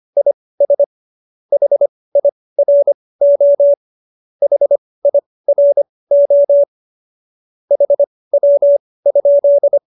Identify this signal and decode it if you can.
Morse code, heard as a keyed tone, about 25 wpm, spaced as for 18 wpm, 570 Hz. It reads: IS HIRO HIRO HW?